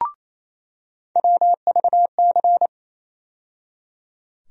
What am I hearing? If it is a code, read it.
Morse code, heard as a keyed tone, about 28 wpm, 700 Hz. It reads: WVC
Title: USA Call Sign Postfixes - 3 Letters 28wpm